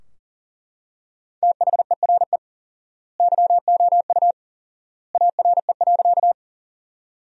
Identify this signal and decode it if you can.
Morse code, heard as a keyed tone, about 40 wpm, 700 Hz. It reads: THERE YOU ARE.
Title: 4x Word-Spacing - Sentences from Top 100 Words - 40wpm